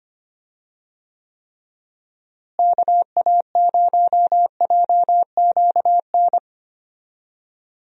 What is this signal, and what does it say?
Morse code, heard as a keyed tone, about 25 wpm, 700 Hz. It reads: KA0JQN
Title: USA Call Signs - 6 Characters - Repeated 3x - 25wpm